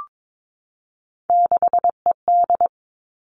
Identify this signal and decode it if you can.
Morse code, heard as a keyed tone, about 22 wpm, 700 Hz. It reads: 6ED
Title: Three Challenging Letter-Number Combinations 22wpm